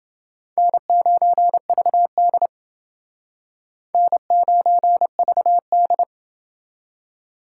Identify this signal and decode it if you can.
Morse code, heard as a keyed tone, about 29 wpm, 700 Hz. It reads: N9VD N9VD